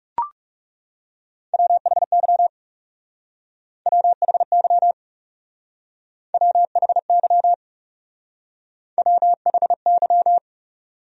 Morse code, heard as a keyed tone, about 40 wpm, 700 Hz. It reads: WHY WHY WHY WHY